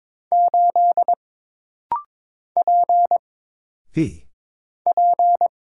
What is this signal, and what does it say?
Morse code, heard as a keyed tone, about 22 wpm, 700 Hz. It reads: 8 P P
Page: This rapid-fire timing is geared to push your ICR (Instant Character Recognition) skills.